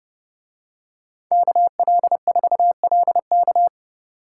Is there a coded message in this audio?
Yes: Morse code from 1.3 to 3.7 s, heard as a keyed tone, about 31 wpm, 700 Hz. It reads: KL4LK